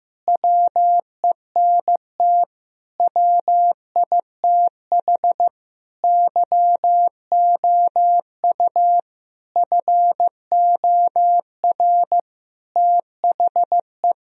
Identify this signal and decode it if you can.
Morse code, heard as a keyed tone, about 15 wpm, 700 Hz. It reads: WENT WITH YOU FOR THE
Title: Sentences from Top 1000 Words - Review Part 1 - 15wpm